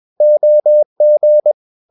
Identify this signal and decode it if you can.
Morse code, heard as a keyed tone, about 21 wpm, 600 Hz. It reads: OG
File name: Ferie QRS 21wpm